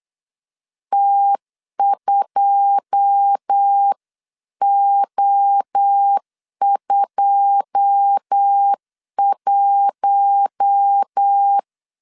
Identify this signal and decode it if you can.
Morse code, heard as a keyed tone, about 8 wpm, 800 Hz. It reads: T2O21
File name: CW / DET / 1 9 1500C S C 8wpm 5653